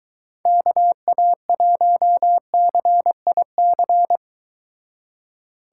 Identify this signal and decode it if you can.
Morse code, heard as a keyed tone, about 23 wpm, 700 Hz. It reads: KA1CIC